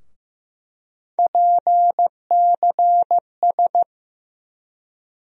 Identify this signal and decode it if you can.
Morse code, heard as a keyed tone, about 16 wpm, 700 Hz. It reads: PCS
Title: USA Call Sign Postfixes - 3 Letters 15wpm